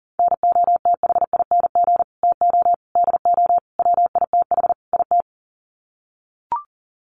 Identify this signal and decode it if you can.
Morse code, heard as a keyed tone, about 40 wpm, 700 Hz. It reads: NOTHING TO DO WITH IT
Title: Sets of 5 Words Encore 40wpm